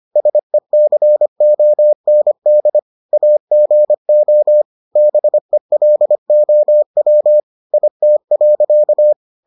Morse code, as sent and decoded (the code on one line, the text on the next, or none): ... -.-. --- -. -.. / .- --. --- / -... .-.. --- .-- / .. - .-.-.-
SECOND AGO BELOW IT.